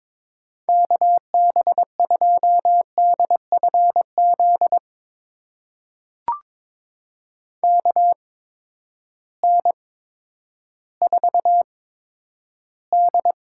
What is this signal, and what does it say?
Morse code, heard as a keyed tone, about 22 wpm, spaced as for 11 wpm, 700 Hz. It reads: KB2DFZ KN4D
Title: USA Call Signs - 6 Characters - 8x Character Spacing - 22wpm